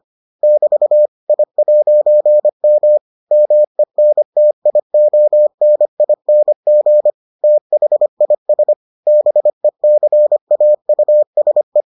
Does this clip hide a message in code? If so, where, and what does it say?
Yes: Morse code from 0.4 to 11.8 s, heard as a keyed tone, about 25 wpm, 600 Hz. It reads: <BT> I'M MENTIONING THIS BECAUSE